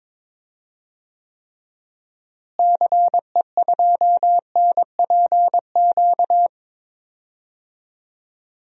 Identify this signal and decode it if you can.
Morse code, heard as a keyed tone, about 22 wpm, 700 Hz. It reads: CE2NPQ
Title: DXCC-100 Call Signs - 6 Characters - Repeated 3x - 22wpm